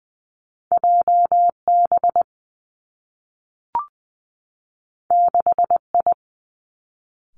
Morse code, heard as a keyed tone, about 20 wpm, 700 Hz. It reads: J B 6 I
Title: Two Challenging Letter-Number Combinations 20wpm